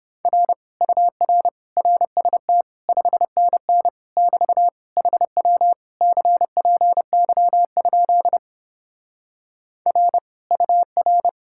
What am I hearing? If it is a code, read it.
Morse code, heard as a keyed tone, about 31 wpm, 700 Hz. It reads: R UR RST 5NN <BT> HW CPY? R UR